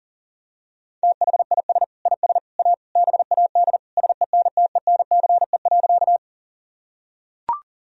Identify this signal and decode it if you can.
Morse code, heard as a keyed tone, about 40 wpm, 700 Hz. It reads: THIS IS A BAD SENTENCE.
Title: Sentences from Top 1000 Words - Review Part 1 - 40wpm